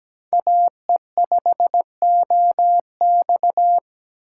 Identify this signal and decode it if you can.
Morse code, heard as a keyed tone, about 17 wpm, 700 Hz. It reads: AE5OX
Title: USA Call Signs - 5 Characters - 4x Character Spacing - 17wpm